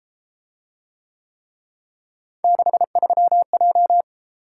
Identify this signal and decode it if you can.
Morse code, heard as a keyed tone, about 33 wpm, 700 Hz. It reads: 63J